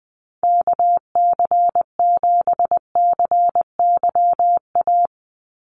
Morse code, heard as a keyed tone, about 20 wpm, 700 Hz. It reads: KC7CYA
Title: USA Call Signs 20wpm